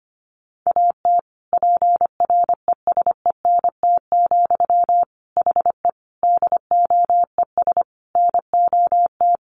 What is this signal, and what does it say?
Morse code, heard as a keyed tone, about 25 wpm, 700 Hz. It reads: AT PRESENT, HE DOES NOT
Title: Sentences from Top 800 Words 25wpm